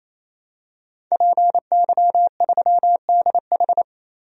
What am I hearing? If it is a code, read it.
Morse code, heard as a keyed tone, about 28 wpm, 700 Hz. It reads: PY3DH
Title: DXCC-100 Call Signs - 5 Characters - 28wpm